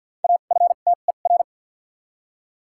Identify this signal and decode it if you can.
Morse code, heard as a keyed tone, about 53 wpm, 700 Hz. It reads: A F T E R
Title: 2x Character Spacing - Top 300 Words - 50wpm